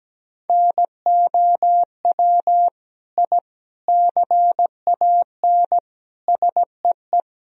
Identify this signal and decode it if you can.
Morse code, heard as a keyed tone, about 17 wpm, 700 Hz. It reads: NOW I CAN SEE
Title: Sentences from Top 1000 Words - Review Part 3 - 17wpm